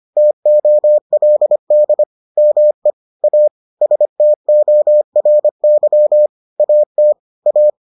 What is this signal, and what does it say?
Morse code, heard as a keyed tone, about 25 wpm, 600 Hz. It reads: TOLD ME A STORY AT A